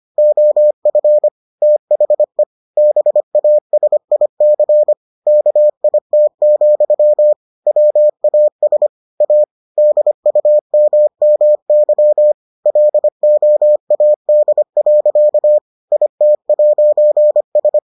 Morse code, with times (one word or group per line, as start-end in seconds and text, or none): 0.2-1.3 OF
1.6-2.4 THE
2.8-4.9 BASIC
5.3-7.3 KIT,
7.7-8.9 WAS
9.2-9.4 A
9.8-12.3 DUMMY
12.7-15.6 LOAD.
15.9-17.8 IT'S